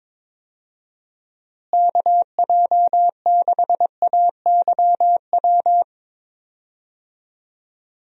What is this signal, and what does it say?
Morse code, heard as a keyed tone, about 22 wpm, 700 Hz. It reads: KJ6AYW
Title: USA Call Signs - 6 Characters - Repeated 3x - 22wpm